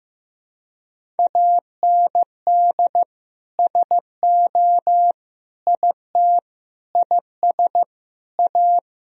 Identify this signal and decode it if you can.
Morse code, heard as a keyed tone, about 15 wpm, 700 Hz. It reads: AND SO IT IS A
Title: Sentences from Top 1000 Words - Review Part 3 - 15wpm